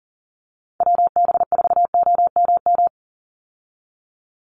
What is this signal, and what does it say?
Morse code, heard as a keyed tone, about 40 wpm, 700 Hz. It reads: WB4OMM
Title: Call Signs Repeated 2x - 40wpm